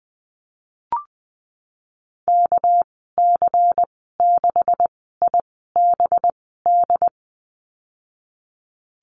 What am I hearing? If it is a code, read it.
Morse code, heard as a keyed tone, about 19 wpm, spaced as for 14 wpm, 700 Hz. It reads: KC6IBD
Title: 213 Lesson 34g - Number 8 - Call Signs 20wpm